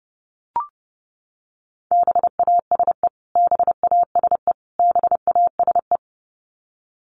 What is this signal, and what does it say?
Morse code, heard as a keyed tone, about 30 wpm, 700 Hz. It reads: BASE BASE BASE
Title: Top 500 Words Repeated 30wpm